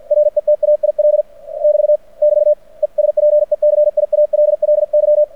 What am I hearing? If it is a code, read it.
Morse code, heard as a keyed tone, about 49 wpm, 600 Hz. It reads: RETAIL <BT> <BT> ESPECIALLY
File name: archived / CRSnet